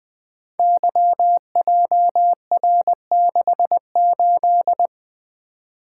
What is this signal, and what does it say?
Morse code, heard as a keyed tone, about 20 wpm, 700 Hz. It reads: YJR68
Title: Code Groups - 5 Letter:Numbers - 20wpm